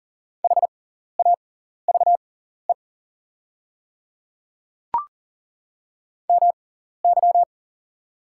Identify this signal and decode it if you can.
Morse code, heard as a keyed tone, about 40 wpm, spaced as for 14 wpm, 700 Hz. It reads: HAVE MY
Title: 6x Character Spacing - Top 100 Words - 40wpm